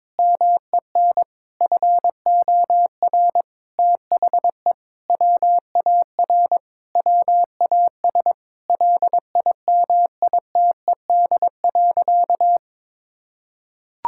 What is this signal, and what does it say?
Morse code, heard as a keyed tone, about 21 wpm, 700 Hz. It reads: MEN FOR THE WAR WAS LIMITED.
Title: Sentences from Top 1000 Words - Review Part 6 - 22wpm